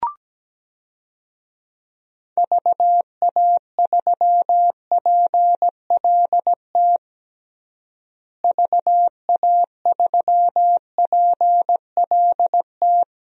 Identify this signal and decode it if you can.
Morse code, heard as a keyed tone, about 17 wpm, 700 Hz. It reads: VA3PLT VA3PLT
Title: DXCC-100 Call Signs - 6 Characters - Repeated 3x - 17wpm